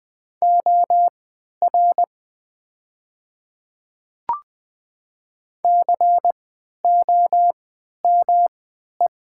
Morse code, heard as a keyed tone, about 20 wpm, spaced as for 11 wpm, 700 Hz. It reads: OR COME
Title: 3x Character Spacing - Top 100 Words - 20wpm